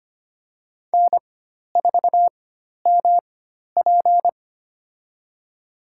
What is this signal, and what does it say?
Morse code, heard as a keyed tone, about 25 wpm, spaced as for 12 wpm, 700 Hz. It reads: N4MP
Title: USA Call Signs - 4 Characters - 4x Character Spacing - 25wpm